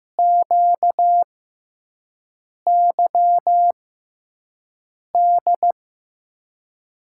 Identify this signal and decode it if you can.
Morse code, heard as a keyed tone, about 15 wpm, spaced as for 5 wpm, 700 Hz. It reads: QYD
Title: USA Call Signs - 6 Characters - 6x Character Spacing - 15wpm